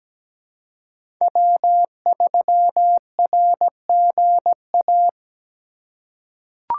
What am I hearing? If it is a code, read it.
Morse code, heard as a keyed tone, about 17 wpm, 700 Hz. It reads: W3RGA
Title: K1USN SST QSO Practice - 17wpm